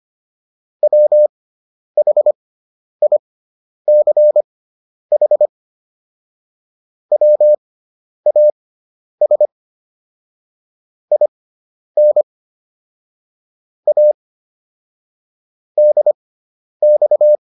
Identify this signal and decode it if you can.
Morse code, heard as a keyed tone, about 25 wpm, spaced as for 10 wpm, 600 Hz. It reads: WHICH WAS IN A DX